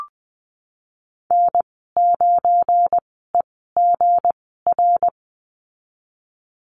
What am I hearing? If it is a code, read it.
Morse code, heard as a keyed tone, about 20 wpm, spaced as for 15 wpm, 700 Hz. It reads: N9EGR
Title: 186 Lesson 30g - Letter b - Call Signs 20wpm